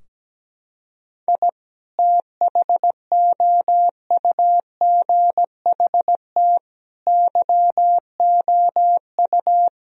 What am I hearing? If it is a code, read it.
Morse code, heard as a keyed tone, about 17 wpm, 700 Hz. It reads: I THOUGHT YOU
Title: Sentences from Top 1000 Words - Review Part 4 - 17wpm